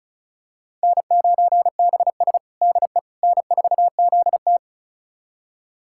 Morse code, heard as a keyed tone, about 34 wpm, 700 Hz. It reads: N9BS DE N4ZT
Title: USA Callsign DE Callsign - 4 Characters - 35wpm